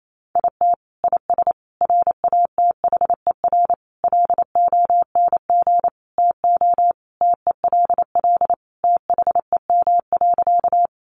Morse code, heard as a keyed tone, about 28 wpm, 700 Hz. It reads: IT IS RATHER LONG TO TELL THEM.
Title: Sentences from Top 700 Words 28wpm